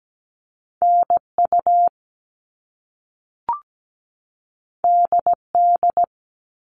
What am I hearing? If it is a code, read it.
Morse code, heard as a keyed tone, about 17 wpm, 700 Hz. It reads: NU DD
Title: All Cut Numbers - 2-Digit 17wpm